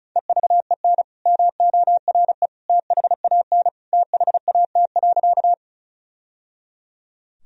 Morse code, heard as a keyed tone, about 36 wpm, 700 Hz. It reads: EVEN MORE THAN THAT.
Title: Sentences from Top 1000 Words - Review Part 1 - 35wpm